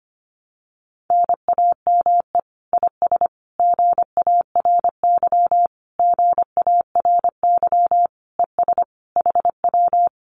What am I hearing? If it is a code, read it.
Morse code, heard as a keyed tone, about 25 wpm, 700 Hz. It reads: NAME IS GARY GARY ES HW
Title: Common QSO Phrases Encore 25wpm